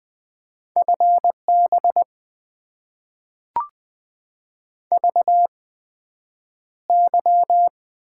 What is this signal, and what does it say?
Morse code, heard as a keyed tone, about 20 wpm, spaced as for 9 wpm, 700 Hz. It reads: FB VY
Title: Top 90 QSO Elements - 8x Character Spacing - 20wpm